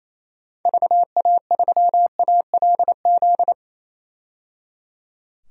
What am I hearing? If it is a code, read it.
Morse code, heard as a keyed tone, about 27 wpm, 700 Hz. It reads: VA3ALZ
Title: DXCC-100 Call Signs - 6 Characters - 28wpm